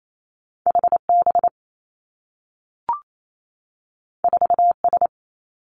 Morse code, heard as a keyed tone, about 28 wpm, 700 Hz. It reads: H B 4 S
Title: Two Challenging Letter-Number Combinations 28wpm